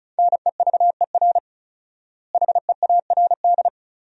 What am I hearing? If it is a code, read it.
Morse code, heard as a keyed tone, about 34 wpm, 700 Hz. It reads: NEVER HEARD
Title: 4x Word-Spacing - Sentences from Top 400 Words - 35wpm